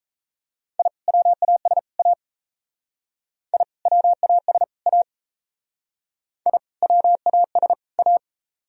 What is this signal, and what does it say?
Morse code, heard as a keyed tone, about 35 wpm, 700 Hz. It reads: I WAS A I WAS A I WAS A
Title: Speed-Racing - Sets of 3 Words 28wpm